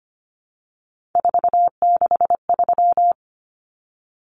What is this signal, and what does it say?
Morse code, heard as a keyed tone, about 25 wpm, 700 Hz. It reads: 463
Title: Common Cut Numbers - 3-Digit 25wpm